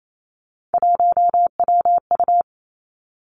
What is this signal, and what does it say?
Morse code, heard as a keyed tone, about 28 wpm, 700 Hz. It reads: 1WU
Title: Three Challenging Letter-Number Combinations 28wpm